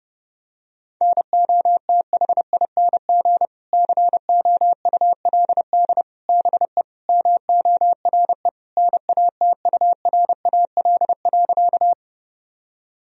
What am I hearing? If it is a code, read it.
Morse code, heard as a keyed tone, about 30 wpm, 700 Hz. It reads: NOTHING COULD BE MORE NATURAL.
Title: Sentences from Top 1000 Words - Review Part 4 - 30wpm